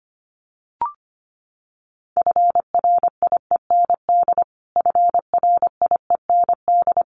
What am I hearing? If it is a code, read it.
Morse code, heard as a keyed tone, about 25 wpm, 700 Hz. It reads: FRIEND FRIEND
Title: Top 500 Words Repeated 25wpm